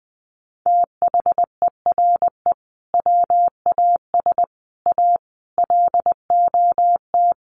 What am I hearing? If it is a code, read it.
Morse code, heard as a keyed tone, about 20 wpm, 700 Hz. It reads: THERE WAS A LOT
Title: Sets of 5 Words 20wpm